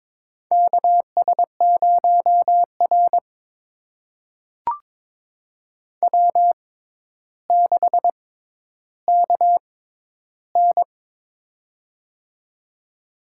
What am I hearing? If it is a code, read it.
Morse code, heard as a keyed tone, about 22 wpm, spaced as for 11 wpm, 700 Hz. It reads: KS0R W6KN
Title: USA Call Signs - 4 Characters - 6x Character Spacing - 22wpm